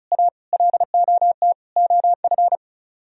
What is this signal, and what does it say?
Morse code, heard as a keyed tone, about 42 wpm, 700 Hz. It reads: A LOT OF
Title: Speed-Racing - Sets of 4 Words 35wpm